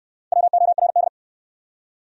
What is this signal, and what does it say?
Morse code, heard as a keyed tone, about 65 wpm, 700 Hz. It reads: WORD